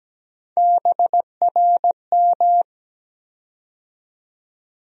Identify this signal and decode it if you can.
Morse code, heard as a keyed tone, about 17 wpm, 700 Hz. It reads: BRM
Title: USA Call Sign Postfixes - 3 Letters 17wpm